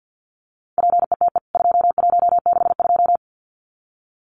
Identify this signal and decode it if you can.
Morse code, heard as a keyed tone, about 49 wpm, 700 Hz. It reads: PETE 2162
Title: CWT Exchange 50wpm